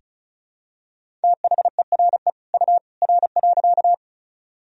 Morse code, heard as a keyed tone, about 36 wpm, 700 Hz. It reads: THERE U R.
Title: Sentences from Top 100 Words and Abbreviations 35wpm